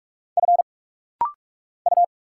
Morse code, heard as a keyed tone, about 46 wpm, 700 Hz. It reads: F U
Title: Single Letters - Rapid-Fire 45wpm